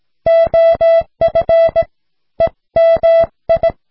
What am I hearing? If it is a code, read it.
Morse code, heard as a keyed tone, about 17 wpm, 660 Hz. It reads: OF EMI